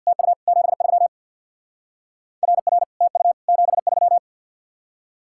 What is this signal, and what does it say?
Morse code, heard as a keyed tone, about 64 wpm, 700 Hz. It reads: TU 73 RR TU 73